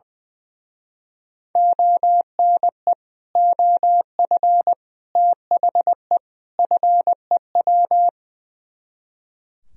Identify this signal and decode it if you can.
Morse code, heard as a keyed tone, about 20 wpm, 700 Hz. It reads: ONE OF THE FEW
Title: Speed-Racing - Sets of 4 Words 20wpm